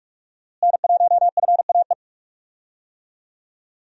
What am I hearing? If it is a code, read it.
Morse code, heard as a keyed tone, about 44 wpm, 700 Hz. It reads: N1FAE
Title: USA Call Signs - 5 Characters - Repeated 3x - 45wpm